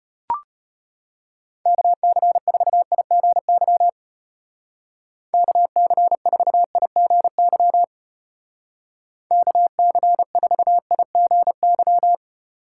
Morse code, heard as a keyed tone, about 32 wpm, 700 Hz. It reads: KC4IGY KC4IGY KC4IGY